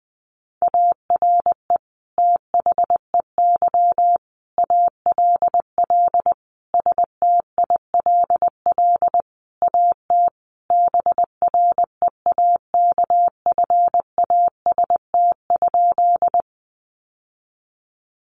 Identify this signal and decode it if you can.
Morse code, heard as a keyed tone, about 20 wpm, 700 Hz. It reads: ARE THEY ALL STILL AT BREAKFAST?
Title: Sentences from Top 1500 Words 20wpm